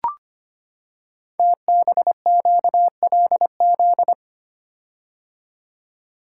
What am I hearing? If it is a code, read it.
Morse code, heard as a keyed tone, about 25 wpm, 700 Hz. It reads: TBQLZ
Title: Code Groups - 5 Letter:Numbers - 25wpm